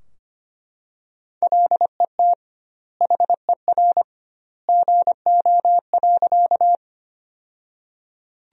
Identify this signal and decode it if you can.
Morse code, heard as a keyed tone, about 25 wpm, 700 Hz. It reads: LET HER GO.
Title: 2x Word-Spacing - Sentences from Top 300 Words - 25wpm